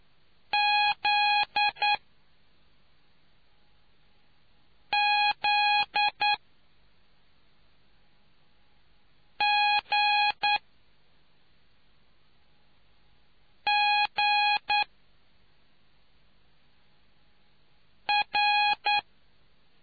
Morse code, heard as a keyed tone, about 9 wpm, spaced as for 3 wpm, 830 Hz. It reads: ZZGGR